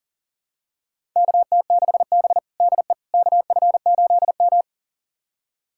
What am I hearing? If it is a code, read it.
Morse code, heard as a keyed tone, about 37 wpm, 700 Hz. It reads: KT6B DE KF8M
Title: USA Callsign DE Callsign - 4 Characters - 40wpm